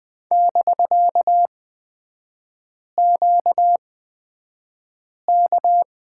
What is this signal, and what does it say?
Morse code, heard as a keyed tone, about 20 wpm, spaced as for 5 wpm, 700 Hz. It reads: <BK>QK
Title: 250 Lesson 40e - BK - Three Characters 20wpm